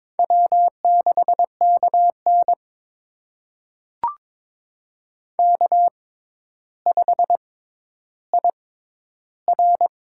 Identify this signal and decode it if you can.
Morse code, heard as a keyed tone, about 22 wpm, spaced as for 11 wpm, 700 Hz. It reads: W6KN K5IR
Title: USA Call Signs - 4 Characters - 6x Character Spacing - 22wpm